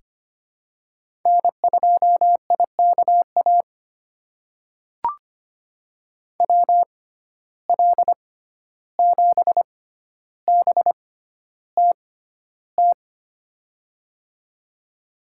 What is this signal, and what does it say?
Morse code, heard as a keyed tone, about 25 wpm, spaced as for 12 wpm, 700 Hz. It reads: N2IKA WL7BTT
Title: USA Call Signs - 6x Character Spacing - 25wpm